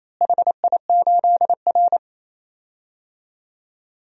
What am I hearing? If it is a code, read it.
Morse code, heard as a keyed tone, about 28 wpm, 700 Hz. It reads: HI8R
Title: DXCC-100 Call Signs - 4 Characters - Repeated 2x - 28wpm